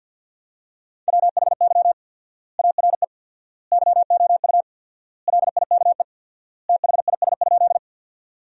Morse code, heard as a keyed tone, about 50 wpm, 700 Hz. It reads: WHY ARE YOU LIKE THIS?
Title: 4x Word-Spacing - Sentences from Top 200 Words - 50wpm